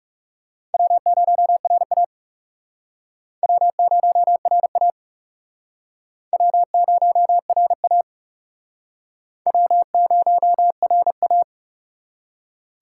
Morse code, heard as a keyed tone, about 40 wpm, 700 Hz. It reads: W0RA W0RA W0RA W0RA